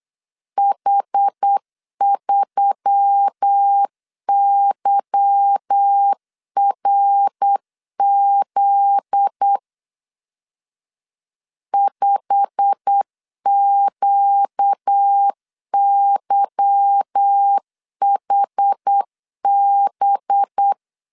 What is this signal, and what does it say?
Morse code, heard as a keyed tone, about 8 wpm, 800 Hz. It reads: H3YRZ 5QYHB
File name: LEZIONI / 10 SIGNS / 10C DOT 8wpm 3649